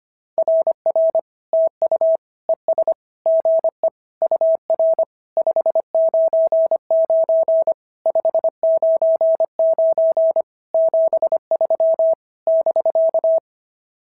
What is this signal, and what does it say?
Morse code, heard as a keyed tone, about 25 wpm, 690 Hz. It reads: RR TU ES GE UR 599 599 73 <BK>
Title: SOTA QSO Practice 25wpm